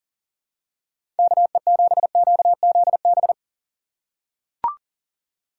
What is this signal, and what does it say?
Morse code, heard as a keyed tone, about 40 wpm, 700 Hz. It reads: KE7QZB
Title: USA Call Signs - 6 Characters - 4x Character Spacing - 40wpm